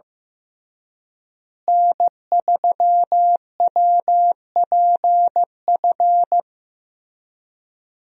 Morse code, heard as a keyed tone, about 15 wpm, 700 Hz. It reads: N3WPF
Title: USA Call Signs - 5 Characters - Repeated 3x - 15wpm